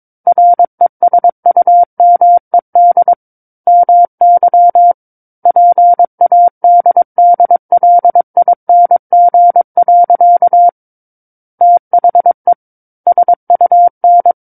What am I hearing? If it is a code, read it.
Morse code, heard as a keyed tone, about 22 wpm, 700 Hz. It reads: RESUMED MY PADDLING. THE SUN
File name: Book One / 13-Chapter 13 - 22 WPM